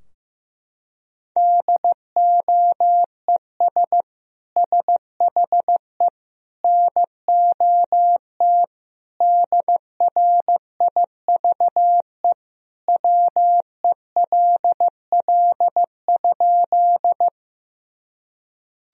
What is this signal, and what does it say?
Morse code, heard as a keyed tone, about 15 wpm, 700 Hz. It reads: DOES SHE NOT DRIVE WELL?
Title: Sentences from Top 1000 Words - Review Part 2 - 15wpm